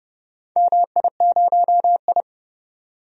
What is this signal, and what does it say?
Morse code, heard as a keyed tone, about 31 wpm, 700 Hz. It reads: MI0I
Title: DXCC-100 Call Signs - 4 Characters - 30wpm